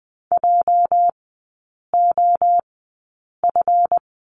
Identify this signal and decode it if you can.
Morse code, heard as a keyed tone, about 20 wpm, spaced as for 8 wpm, 690 Hz. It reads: JOF
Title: 212 Lesson 34e - Number 8 - Three Characters 20wpm